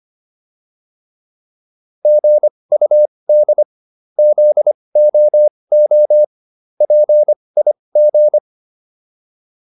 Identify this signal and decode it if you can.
Morse code, heard as a keyed tone, about 25 wpm, spaced as for 20 wpm, 600 Hz. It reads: GUD ZOO PIG